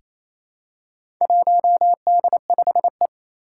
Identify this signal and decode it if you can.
Morse code, heard as a keyed tone, about 27 wpm, 700 Hz. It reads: 1D5E
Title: Four Challenging Letter-Number Combinations 28wpm